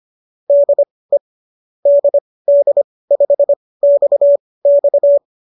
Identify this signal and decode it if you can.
Morse code, heard as a keyed tone, about 25 wpm, spaced as for 18 wpm, 590 Hz. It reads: DE DD5XX
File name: qso218 18